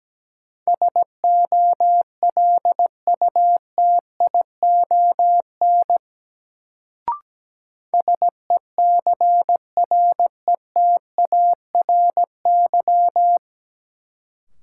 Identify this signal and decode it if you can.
Morse code, heard as a keyed tone, about 17 wpm, 700 Hz. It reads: SOLUTION SECRETARY
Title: Sentences from Top 1300 Words - New Word Drill Alt - 17wpm